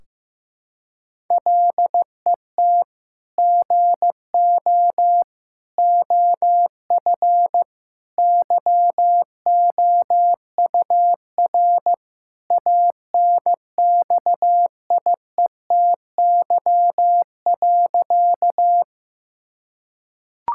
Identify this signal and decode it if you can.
Morse code, heard as a keyed tone, about 15 wpm, 700 Hz. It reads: LET GO OF YOUR ANXIETY.
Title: Sentences from Top 1600 Words 15wpm